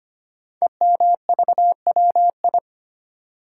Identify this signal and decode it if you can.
Morse code, heard as a keyed tone, about 25 wpm, 700 Hz. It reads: EMVWI
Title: Code Groups - 5 Letters - 25wpm